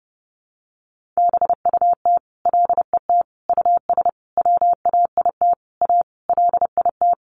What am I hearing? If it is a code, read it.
Morse code, heard as a keyed tone, about 30 wpm, 700 Hz. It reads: BUT LET US WAIT A LIT
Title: Sentences from Top 1500 Words 30wpm